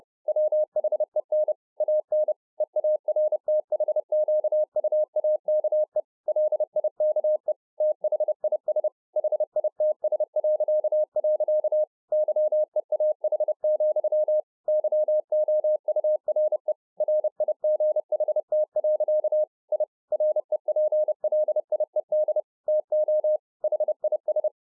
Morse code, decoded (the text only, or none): WHEN AN EARTHQUAKE LIKE THIS HITS.. YEAH, YOURE RIGHT. I REPLIED TO HIS